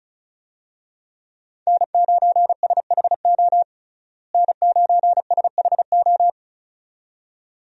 Morse code, heard as a keyed tone, about 35 wpm, 700 Hz. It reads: N9SHO N9SHO
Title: USA Call Signs - 5 Characters - Repeated 2x - 35wpm